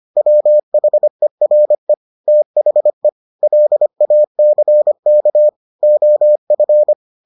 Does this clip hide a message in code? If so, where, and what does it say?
Yes: Morse code from 0.2 to 6.9 s, heard as a keyed tone, about 25 wpm, 600 Hz. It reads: WHERE THE LACK OF